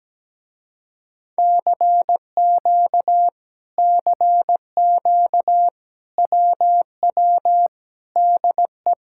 Morse code, heard as a keyed tone, about 17 wpm, 700 Hz. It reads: CQ CQ WW DE
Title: Speed-Racing Common QSO Phrases - 17wpm